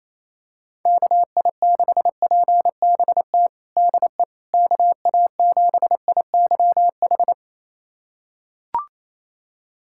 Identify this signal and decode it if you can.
Morse code, heard as a keyed tone, about 28 wpm, 700 Hz. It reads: KI6PBT DE KA7IYH
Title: USA Callsign DE Callsign - 6 Characters - 28wpm